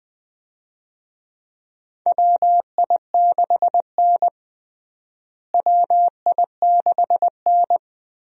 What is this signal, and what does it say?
Morse code, heard as a keyed tone, about 20 wpm, 700 Hz. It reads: WI6N WI6N
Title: USA Call Signs - 4 Characters - Repeated 3x - 20wpm